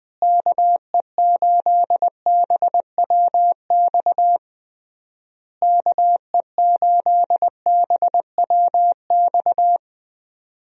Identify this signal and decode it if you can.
Morse code, heard as a keyed tone, about 21 wpm, 700 Hz. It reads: KE8BWX KE8BWX
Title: USA Call Signs - 6 Characters - Repeated 3x - 20wpm